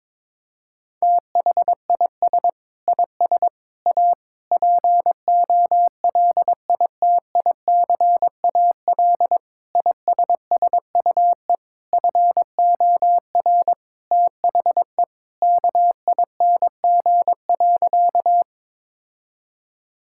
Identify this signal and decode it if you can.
Morse code, heard as a keyed tone, about 22 wpm, 700 Hz. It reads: THIS IS A POLITICAL ISSUE FOR THE KING.
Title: Sentences from Top 1000 Words - Review Part 6 - 22wpm